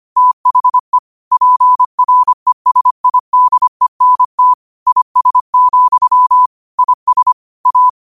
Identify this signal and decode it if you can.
Morse code, heard as a keyed tone, about 25 wpm, 1000 Hz. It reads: THE PRESIDENT IS, IS A
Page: New quotes every day in morse code at 25 Words per minute.